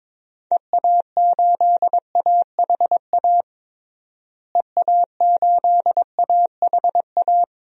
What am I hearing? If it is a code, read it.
Morse code, heard as a keyed tone, about 22 wpm, 700 Hz. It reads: EA8AHA EA8AHA
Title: DXCC-100 Call Signs - 6 Characters - Repeated 2x - 22wpm